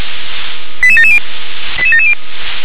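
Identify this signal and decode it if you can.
Five-tone selective call (ZVEI), heard as quick runs of five tones, tones of about 70 ms. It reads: 88770 88770
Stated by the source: ZVEI five-tone call 88738 received by SDR.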